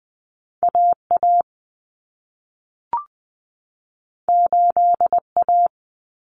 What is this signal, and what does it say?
Morse code, heard as a keyed tone, about 20 wpm, 700 Hz. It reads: AA 8A
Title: Common Cut Numbers - 2-Digit 20wpm